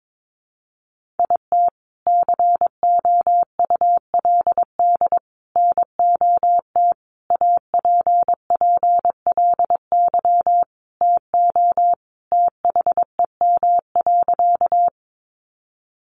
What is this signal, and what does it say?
Morse code, heard as a keyed tone, about 22 wpm, 700 Hz. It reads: IT COULD NOT APPLY TO THEM.
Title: Sentences from Top 1200 Words 22wpm